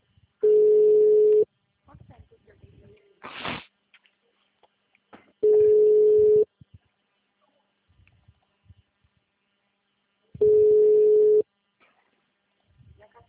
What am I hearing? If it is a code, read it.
Ringback tone (Europe, 425 Hz), with speech.